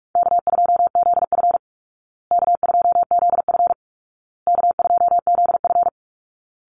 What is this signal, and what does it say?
Morse code, heard as a keyed tone, about 44 wpm, 700 Hz. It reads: K2ZF K2ZF K2ZF
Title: Easy Call Signs Repeated 3x - 45wpm